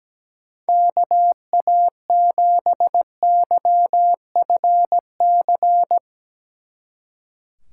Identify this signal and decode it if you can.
Morse code, heard as a keyed tone, about 17 wpm, 700 Hz. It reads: KA7YFC
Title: Speed Racing - USA 2x3 Call Signs 17wpm